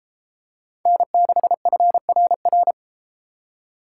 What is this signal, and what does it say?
Morse code, heard as a keyed tone, about 32 wpm, 700 Hz. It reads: N6FRR